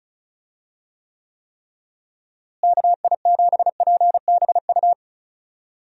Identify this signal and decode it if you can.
Morse code, heard as a keyed tone, about 36 wpm, 700 Hz. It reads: KI7PBU
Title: USA Call Signs - 6 Characters - Repeated 3x - 35wpm